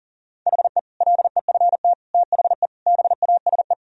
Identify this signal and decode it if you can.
Morse code, heard as a keyed tone, about 40 wpm, 700 Hz. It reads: HE LEFT THE BASE
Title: Sentences from Top 1000 Words - Review Part 1 - 40wpm